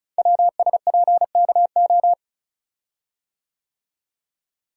Morse code, heard as a keyed tone, about 36 wpm, 690 Hz. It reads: WSPKO
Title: Code Groups - 5 Letters - 35wpm